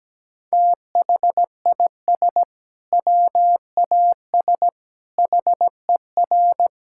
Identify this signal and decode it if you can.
Morse code, heard as a keyed tone, about 17 wpm, 700 Hz. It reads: THIS WAS HER
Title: Sentences from Top 1000 Words - Review Part 6 - 17wpm